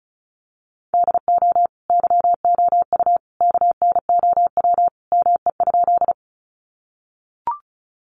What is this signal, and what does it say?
Morse code, heard as a keyed tone, about 34 wpm, 700 Hz. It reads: DO YOU KNOW ME?
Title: Sentences from Top 200 Words 35wpm